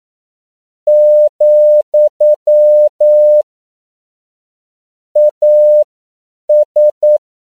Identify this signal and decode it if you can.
Morse code, heard as a keyed tone, about 9 wpm, spaced as for 7 wpm, 600 Hz. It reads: , AS